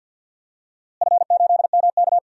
Morse code, heard as a keyed tone, about 50 wpm, 700 Hz. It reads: F8MK
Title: DXCC-100 Call Signs - 4 Characters - Repeated 2x - 50wpm